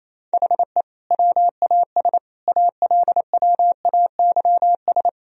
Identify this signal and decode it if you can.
Morse code, heard as a keyed tone, about 27 wpm, 700 Hz. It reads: HE WAS ALWAYS
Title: Sentences from Top 1000 Words - Review Part 2 - 28wpm